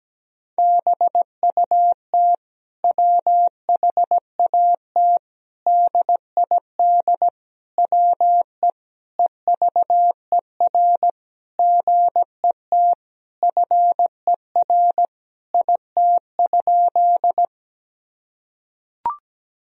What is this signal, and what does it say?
Morse code, heard as a keyed tone, about 17 wpm, 700 Hz. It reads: BUT WHAT DID WE EVER GET FER IT?
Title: Sentences from Top 400 Words and Abbreviations 17wpm